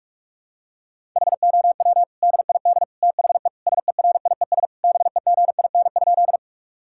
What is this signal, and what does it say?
Morse code, heard as a keyed tone, about 46 wpm, 700 Hz. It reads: HOW DID THE SERIES BEGIN?
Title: Sentences from Top 1000 Words - Review Part 5 - 45wpm